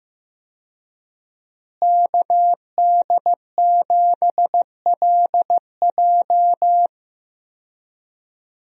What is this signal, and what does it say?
Morse code, heard as a keyed tone, about 16 wpm, 700 Hz. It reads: KD7LJ
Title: USA Call Signs - 5 Characters - Repeated 3x - 15wpm